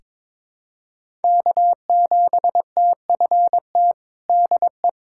Morse code, heard as a keyed tone, about 21 wpm, 700 Hz. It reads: K7TFT DE
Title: USA Callsign DE Callsign - 5 Characters - 22wpm